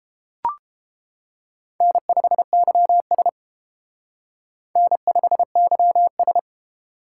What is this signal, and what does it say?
Morse code, heard as a keyed tone, about 32 wpm, 700 Hz. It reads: N5YS N5YS